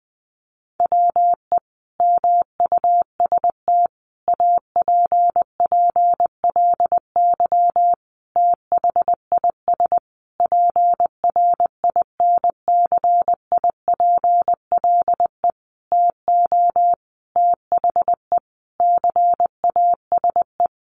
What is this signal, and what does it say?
Morse code, heard as a keyed tone, about 20 wpm, 700 Hz. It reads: WE MUST APPLY THIS PRINCIPLE TO THE CASE
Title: Sentences from Top 1400 Words 20wpm